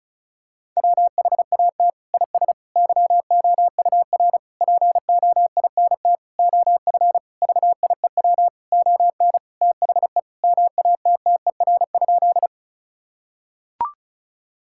Morse code, heard as a keyed tone, about 34 wpm, 700 Hz. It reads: WHAT IS YOUR POINT OF VIEW ON THE MATTER?
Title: Sentences from Top 1000 Words - Review Part 6 - 35wpm